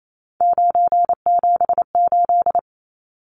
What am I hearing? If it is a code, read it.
Morse code, heard as a keyed tone, about 28 wpm, 700 Hz. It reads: 978
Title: Numbers - 3-Digit 28wpm